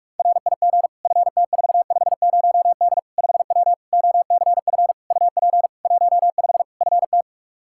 Morse code, heard as a keyed tone, about 46 wpm, 700 Hz. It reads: RIG FT450D 5W OCF UP 15 FT
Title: K1USN SST QSO Practice 45wpm